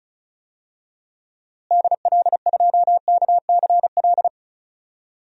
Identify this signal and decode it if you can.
Morse code, heard as a keyed tone, about 36 wpm, 700 Hz. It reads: DL2KCL
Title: DXCC-100 Call Signs - 6 Characters - Repeated 3x - 35wpm